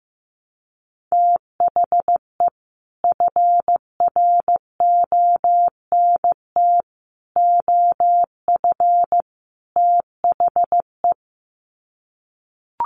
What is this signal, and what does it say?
Morse code, heard as a keyed tone, about 15 wpm, 700 Hz. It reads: THE FRONT OF THE
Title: Sets of 4 Words 15wpm